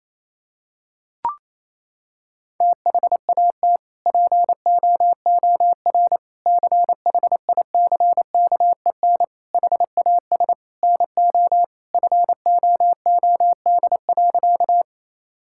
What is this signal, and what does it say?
Morse code, heard as a keyed tone, about 28 wpm, 700 Hz. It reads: THAT POOR CHICKEN HAS NO FOOD.
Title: Sentences from Top 1000 Words - Review Part 5 - 28wpm